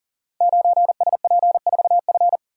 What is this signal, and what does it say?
Morse code, heard as a keyed tone, about 39 wpm, 700 Hz. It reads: 9SP4F
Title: Code Groups - 5 Letter:Numbers - 40wpm